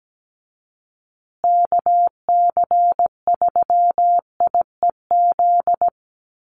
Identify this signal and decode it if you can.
Morse code, heard as a keyed tone, about 17 wpm, 700 Hz. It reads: KC3IEZ
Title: USA Call Signs Repeated 3x 17wpm